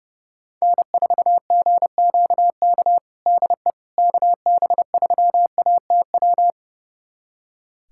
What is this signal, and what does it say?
Morse code, heard as a keyed tone, about 30 wpm, 700 Hz. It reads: N4GQK DE KB3ATW
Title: USA Callsign DE Callsign 30wpm